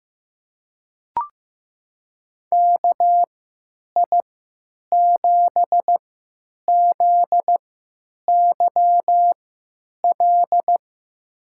Morse code, heard as a keyed tone, about 15 wpm, spaced as for 9 wpm, 700 Hz. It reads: KI7ZYL
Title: USA Call Signs - 3x Character Spacing - 15wpm